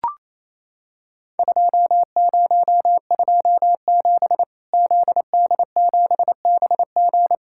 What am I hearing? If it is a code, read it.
Morse code, heard as a keyed tone, about 28 wpm, 700 Hz. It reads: 2027 ZD7BG
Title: Worked All Europe DX Contest - QTC Exchange - 28wpm